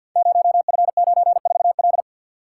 Morse code, heard as a keyed tone, about 50 wpm, 700 Hz. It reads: 0F94L
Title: Code Groups - 5 Letter:Numbers - 50wpm